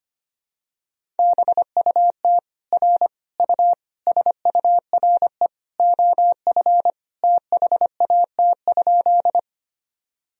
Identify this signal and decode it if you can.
Morse code, heard as a keyed tone, about 25 wpm, 700 Hz. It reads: BUT R U SURE OF THAT?
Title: Sentences from Top 300 Words and Abbreviations 25wpm